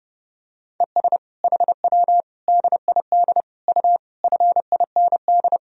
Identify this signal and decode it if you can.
Morse code, heard as a keyed tone, about 30 wpm, 700 Hz. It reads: ES HW DID U FIND
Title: Sentences from Top 100 Words and Abbreviations 30wpm